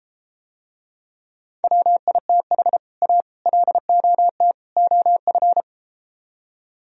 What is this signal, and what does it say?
Morse code, heard as a keyed tone, about 33 wpm, 700 Hz. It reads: WITH A LOT OF